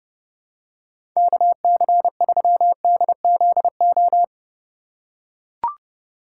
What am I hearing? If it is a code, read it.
Morse code, heard as a keyed tone, about 30 wpm, 700 Hz. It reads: KC3DZO
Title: USA Call Signs - 6 Characters - 4x Character Spacing - 30wpm